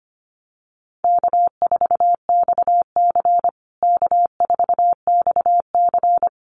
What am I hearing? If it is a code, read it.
Morse code, heard as a keyed tone, about 25 wpm, 700 Hz. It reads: K4XC K4XC
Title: USA Call Signs Repeated 3x 25wpm